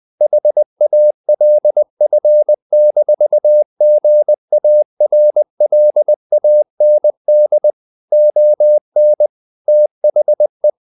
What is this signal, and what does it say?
Morse code, heard as a keyed tone, about 20 wpm, 600 Hz. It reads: HALF-GARLAND ON THE
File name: the old man and the sea / the old man and the sea part 1 20wpm